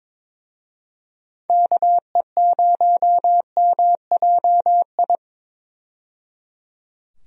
Morse code, heard as a keyed tone, about 22 wpm, 700 Hz. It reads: KE0MJI
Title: USA Call Signs - 6 Characters - 22wpm